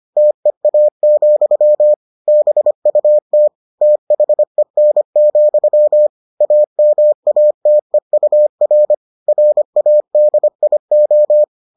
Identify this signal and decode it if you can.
Morse code, heard as a keyed tone, about 25 wpm, 600 Hz. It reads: TEA, BUT THEN, AMATEUR RADIO